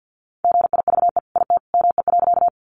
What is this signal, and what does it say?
Morse code, heard as a keyed tone, about 50 wpm, 700 Hz. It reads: GIVE IT ME.
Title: Sentences from Top 200 Words 50wpm